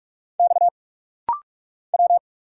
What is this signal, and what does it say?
Morse code, heard as a keyed tone, about 46 wpm, 700 Hz. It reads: X W
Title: Single Letters - Rapid-Fire 45wpm